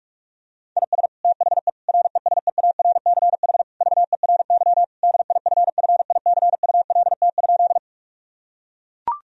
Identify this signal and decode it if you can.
Morse code, heard as a keyed tone, about 41 wpm, 700 Hz. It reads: IS THE RESEARCH VERY DIFFICULT?
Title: Sentences from Top 1000 Words - Review Part 4 - 45wpm